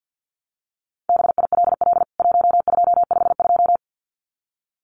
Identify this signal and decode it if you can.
Morse code, heard as a keyed tone, about 50 wpm, 700 Hz. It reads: BILL 1252
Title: CWT Exchange 50wpm